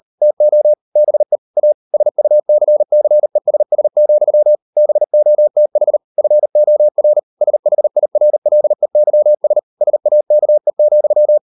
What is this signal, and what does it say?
Morse code, heard as a keyed tone, about 38 wpm, 600 Hz. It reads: TO BE A SUCCESS, BOTH FOR SHIRLEYS SAKE,